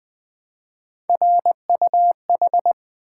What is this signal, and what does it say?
Morse code, heard as a keyed tone, about 20 wpm, 700 Hz. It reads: RUH
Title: Three Letter-Number Combinations - 20wpm